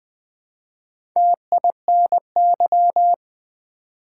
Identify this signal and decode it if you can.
Morse code, heard as a keyed tone, about 20 wpm, 700 Hz. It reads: TINY
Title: Build A Word - 1 Letter At A Time - 4-Letter Words - 20wpm